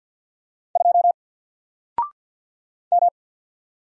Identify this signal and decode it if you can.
Morse code, heard as a keyed tone, about 49 wpm, 700 Hz. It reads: 2 M
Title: Single Character (Letter:Number:Punctuation:Prosign) 50wpm